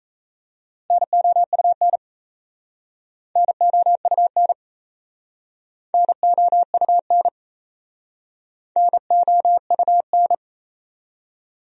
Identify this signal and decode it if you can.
Morse code, heard as a keyed tone, about 37 wpm, 700 Hz. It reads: NOUN NOUN NOUN NOUN